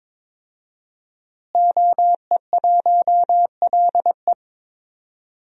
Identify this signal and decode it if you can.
Morse code, heard as a keyed tone, about 21 wpm, 700 Hz. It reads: OE1LE
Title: DXCC-100 Call Signs - 5 Characters - Repeated 3x - 22wpm